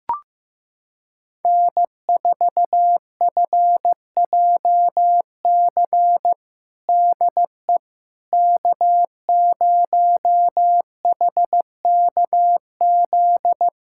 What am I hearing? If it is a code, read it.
Morse code, heard as a keyed tone, about 15 wpm, 700 Hz. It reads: N4FJC DE K0HKZ
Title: USA Callsign DE Callsign - 5 Characters - 15wpm